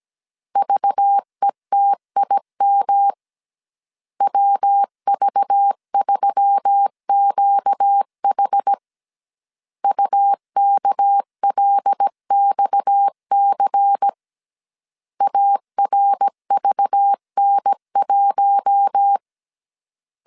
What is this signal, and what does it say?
Morse code, heard as a keyed tone, about 17 wpm, 800 Hz. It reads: VETIM WV3QH UKLXC ARVN1
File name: CW / DET / 1 6 1500C S B 16wpm 2619